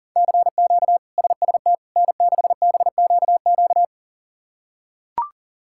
Morse code, heard as a keyed tone, about 40 wpm, 700 Hz. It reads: CQ SST N6BQQ
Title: K1USN SST QSO Practice - 40wpm